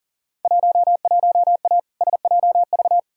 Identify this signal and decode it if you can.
Morse code, heard as a keyed tone, about 40 wpm, 700 Hz. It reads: 11A SJV
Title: Field Day Exchanges Encore - 40wpm